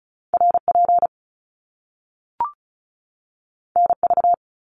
Morse code, heard as a keyed tone, about 36 wpm, 700 Hz. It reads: RP NV
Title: Top Two Letter Combinations in Words 35wpm